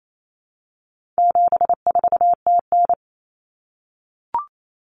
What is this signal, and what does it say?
Morse code, heard as a keyed tone, about 28 wpm, 700 Hz. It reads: 74TN
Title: Common Cut Numbers - 4-Digit 28wpm